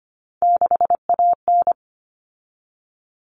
Morse code, heard as a keyed tone, about 25 wpm, 700 Hz. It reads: 6AN
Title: Common Cut Numbers - 3-Digit 25wpm